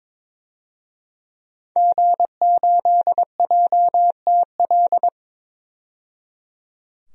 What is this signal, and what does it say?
Morse code, heard as a keyed tone, about 22 wpm, 700 Hz. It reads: G8JTL
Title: DXCC-100 Call Signs - 5 Characters - 22wpm